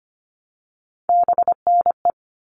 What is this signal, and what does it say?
Morse code, heard as a keyed tone, about 25 wpm, 700 Hz. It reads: BNE
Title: All Cut Numbers - 3-Digit 25wpm